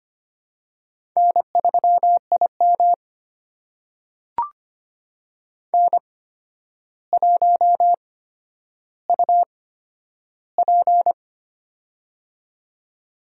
Morse code, heard as a keyed tone, about 25 wpm, spaced as for 11 wpm, 700 Hz. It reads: N3IM N1UP
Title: USA Call Signs - 4 Characters - 8x Character Spacing - 25wpm